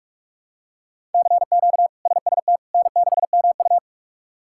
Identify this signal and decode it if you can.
Morse code, heard as a keyed tone, about 45 wpm, 700 Hz. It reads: CQ SST N6MU
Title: K1USN SST QSO Practice - 45wpm